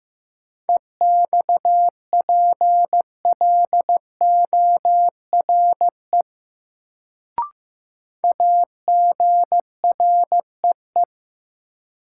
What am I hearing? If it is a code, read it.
Morse code, heard as a keyed tone, about 15 wpm, 700 Hz. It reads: EXPLORE AGREE
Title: Sentences from Top 1100 Words - New Word Drill Alt - 15wpm